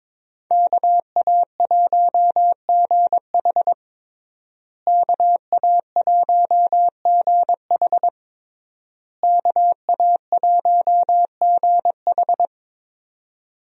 Morse code, heard as a keyed tone, about 22 wpm, 700 Hz. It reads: KA1GH KA1GH KA1GH
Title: USA Call Signs - 5 Characters - Repeated 3x - 22wpm